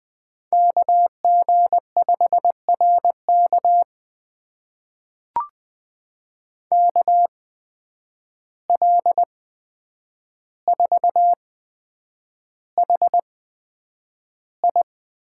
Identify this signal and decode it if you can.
Morse code, heard as a keyed tone, about 20 wpm, spaced as for 9 wpm, 700 Hz. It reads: KG5RK KL4HI
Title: USA Call Signs - 5 Characters - 8x Character Spacing - 20wpm